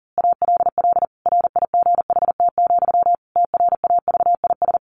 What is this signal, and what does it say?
Morse code, heard as a keyed tone, about 40 wpm, 700 Hz. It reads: ALL RIGHT, TRAVIS
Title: A Sound Of Thunder 40wpm